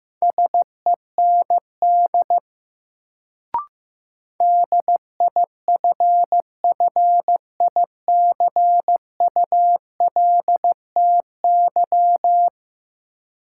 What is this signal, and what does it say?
Morse code, heard as a keyed tone, about 15 wpm, 700 Hz. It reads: SEND DIFFICULTY
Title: Sentences from Top 1500 Words - New Word Drill - 15wpm